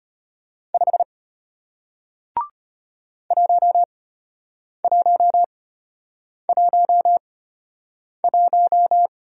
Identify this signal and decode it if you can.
Morse code, heard as a keyed tone, about 34 wpm, spaced as for 8 wpm, 700 Hz. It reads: 5 1111